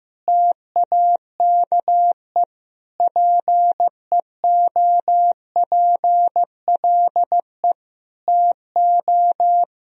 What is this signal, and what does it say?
Morse code, heard as a keyed tone, about 16 wpm, 700 Hz. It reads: TAKE PEOPLE TO
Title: Sentences from Top 1000 Words - Review Part 4 - 15wpm